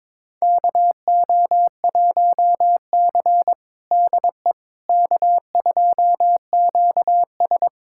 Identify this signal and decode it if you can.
Morse code, heard as a keyed tone, about 22 wpm, 700 Hz. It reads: KO1C DE K2QS
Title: USA Callsign DE Callsign - 4 Characters - 22wpm